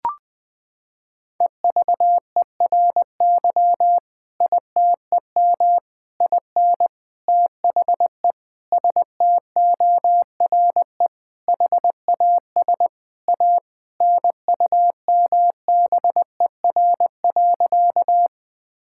Morse code, heard as a keyed tone, about 20 wpm, 700 Hz. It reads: EVERY ITEM IN THE STORE HAS A NUMBER.
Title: Sentences from Top 1000 Words - Review Part 6 - 20wpm